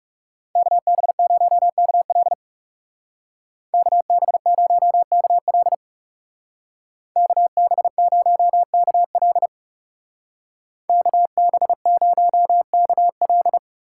Morse code, heard as a keyed tone, about 36 wpm, 700 Hz. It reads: KB0KL KB0KL KB0KL KB0KL
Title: Speed Racing - USA Call Signs 30wpm